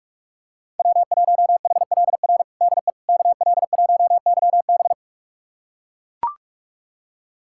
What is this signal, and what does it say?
Morse code, heard as a keyed tone, about 43 wpm, 700 Hz. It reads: W1HLR DE KL1YB
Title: USA Callsign DE Callsign - 5 Characters - 45wpm